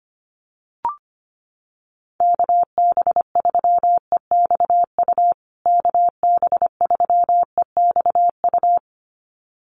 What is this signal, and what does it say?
Morse code, heard as a keyed tone, about 25 wpm, 700 Hz. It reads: KB3EXU KB3EXU
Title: USA Call Signs Repeated 2x 25wpm